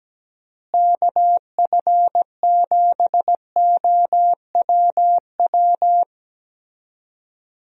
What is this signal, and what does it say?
Morse code, heard as a keyed tone, about 17 wpm, 700 Hz. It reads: KF7OWW
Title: USA Call Signs - 6 Characters - Repeated 3x - 17wpm